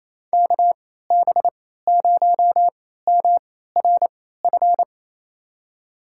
Morse code, heard as a keyed tone, about 27 wpm, spaced as for 16 wpm, 700 Hz. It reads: KB0MRF
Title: USA Call Signs - 3x Character Spacing - 28wpm